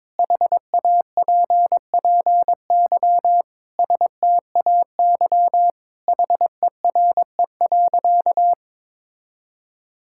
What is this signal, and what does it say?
Morse code, heard as a keyed tone, about 22 wpm, 700 Hz. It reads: HAPPY STAY HERE.
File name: practice / 22wpm / Sentences with 4-Word Phrases - 22wpm